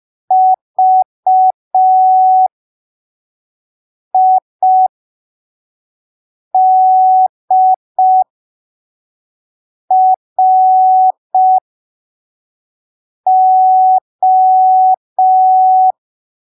Morse code, heard as a keyed tone, about 5 wpm, spaced as for 3 wpm, 750 Hz. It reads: VIDRO
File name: arquivos / texto / texto 11 05ppm